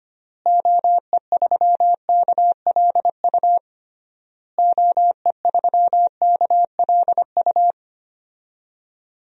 Morse code, heard as a keyed tone, about 25 wpm, 700 Hz. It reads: OE3KLU OE3KLU
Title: DXCC-100 Call Signs - 6 Characters - Repeated 3x - 25wpm